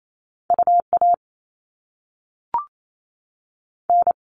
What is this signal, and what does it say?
Morse code, heard as a keyed tone, about 28 wpm, 700 Hz. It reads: U A N
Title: All Cut Numbers - 2-Digit 28wpm